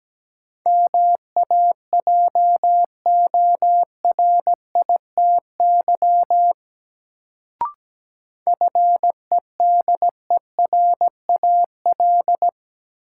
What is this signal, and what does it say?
Morse code, heard as a keyed tone, about 17 wpm, 700 Hz. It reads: MAJORITY FEDERAL
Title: Sets of 3 Words - New Word Drill - 17wpm